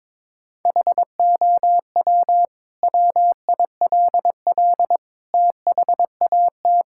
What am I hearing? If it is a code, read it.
Morse code, heard as a keyed tone, about 22 wpm, 700 Hz. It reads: HOW WILL THAT
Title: Sentences from Top 1000 Words - Review Part 6 - 22wpm